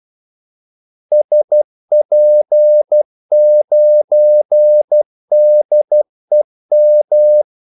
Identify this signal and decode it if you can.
Morse code, heard as a keyed tone, about 12 wpm, 600 Hz. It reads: SP9DEM
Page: 2 ekstra ordmellemrum mellem kaldesignalerne